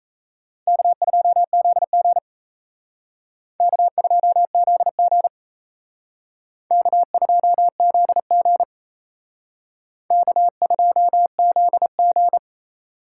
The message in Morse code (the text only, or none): K2ZG K2ZG K2ZG K2ZG